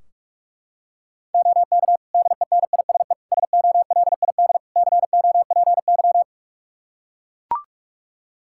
Morse code, heard as a keyed tone, about 41 wpm, 700 Hz. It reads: OK DENISE SOLID COPY
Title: Practice QSO - Exchange 3 - More Info - 45wpm